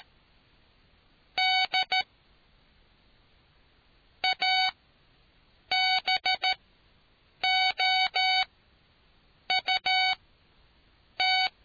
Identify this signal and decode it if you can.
Morse code, heard as a keyed tone, about 13 wpm, spaced as for 6 wpm, 750 Hz. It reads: D ABOUT